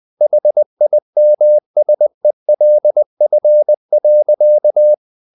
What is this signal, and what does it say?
Morse code, heard as a keyed tone, about 20 wpm, 600 Hz. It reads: HIMSELF.
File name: the old man and the sea / the old man and the sea part 1 20wpm